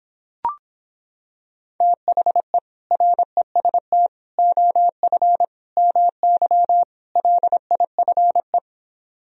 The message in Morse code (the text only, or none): THE REST OF MY LIFE